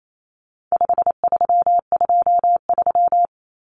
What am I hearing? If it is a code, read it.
Morse code, heard as a keyed tone, about 28 wpm, 700 Hz. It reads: 5323
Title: Numbers - 4-Digit 28wpm